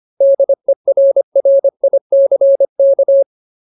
Morse code, heard as a keyed tone, about 25 wpm, 550 Hz. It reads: DERRICK
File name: Ultimate Word List / 25WPM Ultimate